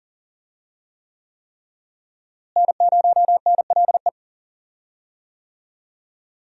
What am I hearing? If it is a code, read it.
Morse code, heard as a keyed tone, about 40 wpm, 700 Hz. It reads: N0NLE
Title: USA Call Signs - 5 Characters - Repeated 3x - 40wpm